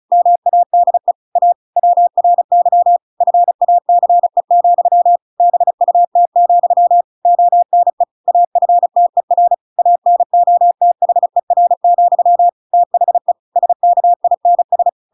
Morse code, heard as a keyed tone, about 34 wpm, 700 Hz. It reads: MADE A WRY FACE, BUT, ONE AFTER ANOTHER, THE SKINS
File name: Pinocchio / Pinocchio ch7 35 35 ik1hge